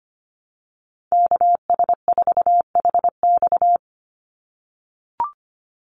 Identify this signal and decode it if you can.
Morse code, heard as a keyed tone, about 25 wpm, 700 Hz. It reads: KS4HX
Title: USA Call Signs 25wpm